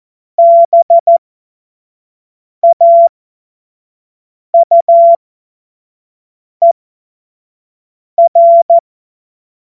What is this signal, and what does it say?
Morse code, heard as a keyed tone, about 14 wpm, spaced as for 5 wpm, 680 Hz. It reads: BAUER
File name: Witz-19-August-2025-25-F70